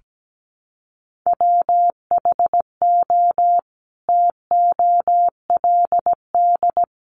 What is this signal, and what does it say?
Morse code, heard as a keyed tone, about 17 wpm, 700 Hz. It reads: WHO TOLD
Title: Sentences from Top 400 Words 17wpm